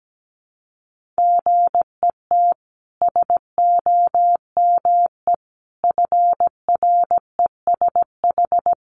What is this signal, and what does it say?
Morse code, heard as a keyed tone, about 17 wpm, 700 Hz. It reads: GET SOME FRESH
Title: Sentences from Top 1200 Words 17wpm